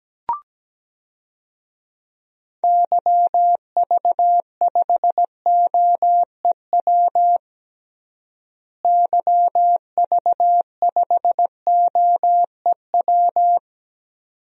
Morse code, heard as a keyed tone, about 17 wpm, 700 Hz. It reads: YV5OEW YV5OEW
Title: DXCC-100 Call Signs - 6 Characters - Repeated 2x - 17wpm